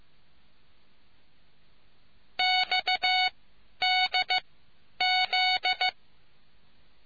Morse code, heard as a keyed tone, about 15 wpm, spaced as for 10 wpm, 740 Hz. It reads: XDZ